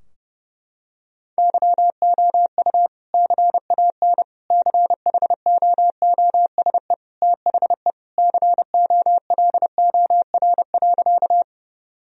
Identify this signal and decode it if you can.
Morse code, heard as a keyed tone, about 30 wpm, 700 Hz. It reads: YOU CAN CHOOSE THE COLOR.
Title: Sentences from Top 1000 Words - Review Part 4 - 30wpm